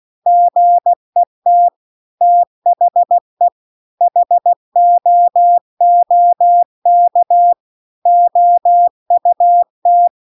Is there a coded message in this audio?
Yes: Morse code from 0.3 to 10.1 s, heard as a keyed tone, about 16 wpm, 700 Hz. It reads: GET THE HOOK OUT